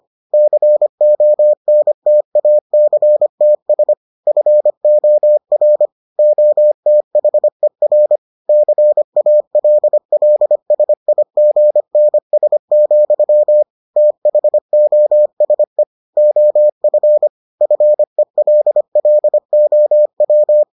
Morse code, heard as a keyed tone, about 25 wpm, 600 Hz. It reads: CONTACTS FOR OTHER CALLSIGNS, THOSE OF FELLOW